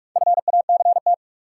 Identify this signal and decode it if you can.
Morse code, heard as a keyed tone, about 46 wpm, 700 Hz. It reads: FACT
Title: Sentences from Top 500 Words - New Word Drill - 45wpm